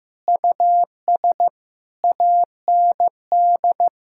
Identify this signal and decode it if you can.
Morse code, heard as a keyed tone, about 15 wpm, 700 Hz. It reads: US AND
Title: Sentences from Top 1000 Words - Review Part 5 - 15wpm